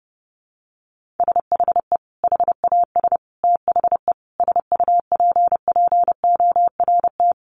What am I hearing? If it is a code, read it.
Morse code, heard as a keyed tone, about 28 wpm, 700 Hz. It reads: SHE HAS THE SUPPORT
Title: Sentences from Top 800 Words 30wpm